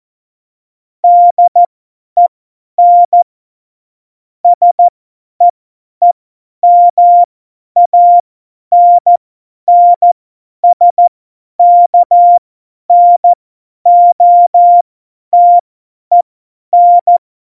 Morse code, heard as a keyed tone, about 14 wpm, spaced as for 10 wpm, 700 Hz. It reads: DEN SEEMANNSKNOTEN
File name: Morsekurs / Witze / Witz 10 April 2018, 50 F70